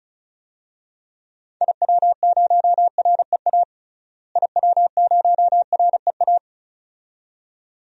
Morse code, heard as a keyed tone, about 36 wpm, 700 Hz. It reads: IW0REA IW0REA
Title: DXCC-100 Call Signs - 6 Characters - Repeated 2x - 35wpm